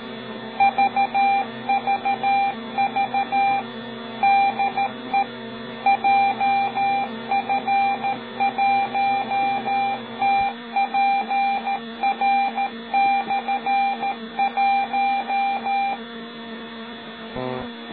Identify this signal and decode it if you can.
Morse code, heard as a keyed tone, about 13 wpm, 790 Hz, with noise. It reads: VVV DE JF1TPR/1
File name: Beacon / 24GidJF1TPR M